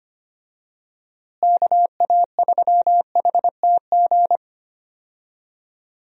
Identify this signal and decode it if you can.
Morse code, heard as a keyed tone, about 25 wpm, 700 Hz. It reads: KA3HTG
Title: USA Call Signs - 6 Characters - Repeated 3x - 25wpm